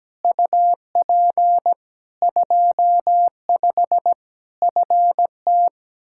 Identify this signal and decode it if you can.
Morse code, heard as a keyed tone, about 17 wpm, 700 Hz. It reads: UP 25 FT
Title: Practice QSO - Exchange 2 - Info - 17wpm